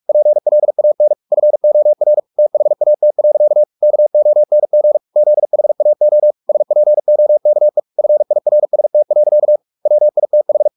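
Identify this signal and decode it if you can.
Morse code, heard as a keyed tone, about 44 wpm, 600 Hz. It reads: PLAN FOR THAT. KONG ZHAO SPOKE FIRST. WITH
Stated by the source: Morse practice at 45 wpm, the novel Never by Ken Follett in English, from CW med Gnister.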